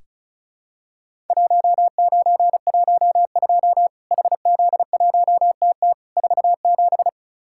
Morse code, heard as a keyed tone, about 36 wpm, 700 Hz. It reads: 1912 HZ1TT 47
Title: Worked All Europe DX Contest - QTC Exchange - 35wpm